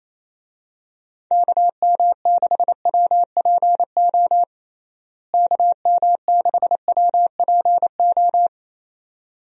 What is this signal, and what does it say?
Morse code, heard as a keyed tone, about 28 wpm, 700 Hz. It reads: KM6WPO KM6WPO
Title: USA Call Signs - 6 Characters - Repeated 2x - 28wpm